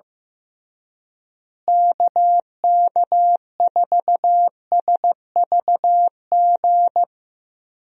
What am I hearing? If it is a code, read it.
Morse code, heard as a keyed tone, about 15 wpm, 700 Hz. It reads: KK4SVG
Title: USA Call Signs - 6 Characters - Repeated 3x - 15wpm